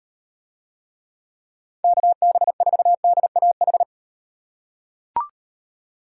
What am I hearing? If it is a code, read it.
Morse code, heard as a keyed tone, about 38 wpm, 700 Hz. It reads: KB4DAH